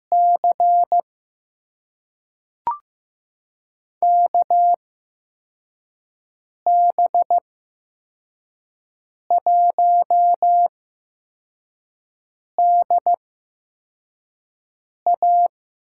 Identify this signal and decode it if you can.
Morse code, heard as a keyed tone, about 15 wpm, spaced as for 4 wpm, 700 Hz. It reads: C KB1DA
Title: USA Call Signs - 6 Characters - 8x Character Spacing - 15wpm